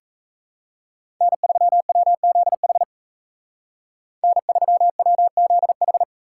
Morse code, heard as a keyed tone, about 32 wpm, 700 Hz. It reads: N3WZH N3WZH
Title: Speed Racing - USA 1x3 Call Signs 28wpm